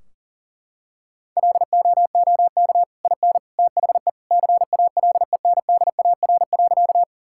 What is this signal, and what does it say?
Morse code, heard as a keyed tone, about 40 wpm, 700 Hz. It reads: LOOK IN THE CALENDAR.
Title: Sentences from Top 1600 Words 40wpm